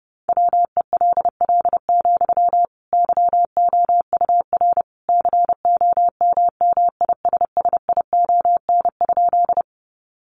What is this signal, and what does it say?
Morse code, heard as a keyed tone, about 30 wpm, 700 Hz. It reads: WELL, YOUR COMMISSION?
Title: Sentences from Top 1400 Words 30wpm